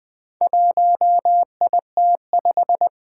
Morse code, heard as a keyed tone, about 20 wpm, 700 Hz. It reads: 1IT5
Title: Four Challenging Letter-Number Combinations 20wpm